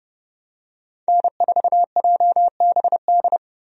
Morse code, heard as a keyed tone, about 30 wpm, 700 Hz. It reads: N4JBD
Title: USA Call Signs - 5 Characters - 30wpm